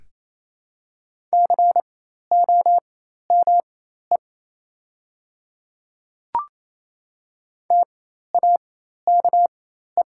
Morse code, heard as a keyed tone, about 28 wpm, spaced as for 13 wpm, 700 Hz. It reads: COME TAKE
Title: 4x Character Spacing - Top 300 Words - 28wpm